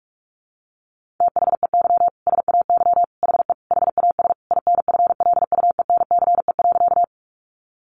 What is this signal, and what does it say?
Morse code, heard as a keyed tone, about 44 wpm, 700 Hz. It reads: THEY SAY HE HAS INFLUENCE.
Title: Sentences from Top 1100 Words 45wpm